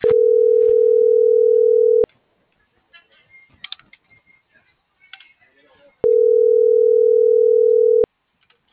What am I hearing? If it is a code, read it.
Ringback tone (North America), 440 + 480 Hz, with speech.
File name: CENTRAL TERRESTRE